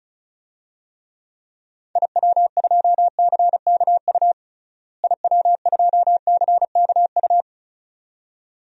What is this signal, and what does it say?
Morse code, heard as a keyed tone, about 34 wpm, 700 Hz. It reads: IW2CKU IW2CKU
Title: DXCC-100 Call Signs - 6 Characters - Repeated 2x - 35wpm